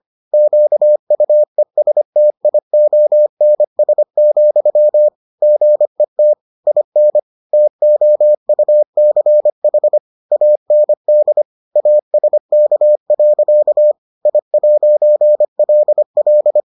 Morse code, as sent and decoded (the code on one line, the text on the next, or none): --.- ..- . ... - .. --- -. ... --..-- / --. - / .. -. / - --- ..- -.-. .... / .- -. -.. / .- ... -.- .-.-.- / .. .----. .-.. .-..
QUESTIONS, GET IN TOUCH AND ASK. I'LL